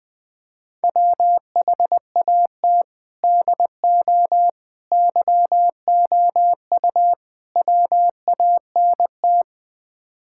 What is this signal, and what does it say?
Morse code, heard as a keyed tone, about 20 wpm, 700 Hz. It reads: WHAT DO YOU WANT
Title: Build A Phrase - 5-Word Phrases - 20wpm